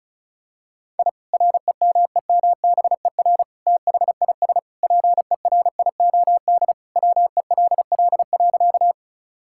Morse code, heard as a keyed tone, about 36 wpm, 700 Hz. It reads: I REMEMBER THIS PERIOD WELL.
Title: Sentences from Top 1000 Words - Review Part 5 - 35wpm